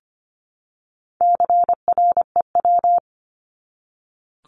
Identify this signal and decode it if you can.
Morse code, heard as a keyed tone, about 25 wpm, 700 Hz. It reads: CREW
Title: Top 1251-1500 Words 25wpm